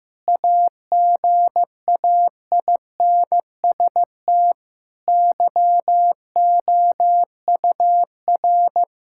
Morse code, heard as a keyed tone, about 15 wpm, 700 Hz. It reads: AGAINST YOUR
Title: Sentences from Top 1000 Words - Review Part 5 - 15wpm